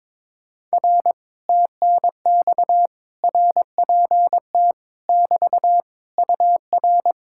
Morse code, heard as a keyed tone, about 22 wpm, 700 Hz. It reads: R TNX RPT <BT> UR
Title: Speed-Racing Common QSO Phrases - 22wpm